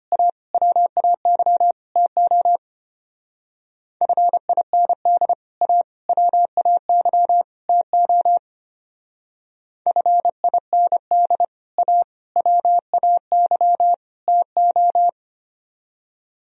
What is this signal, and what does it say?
Morse code, heard as a keyed tone, about 32 wpm, 700 Hz. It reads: A WAY TO FIND A WAY TO FIND A WAY TO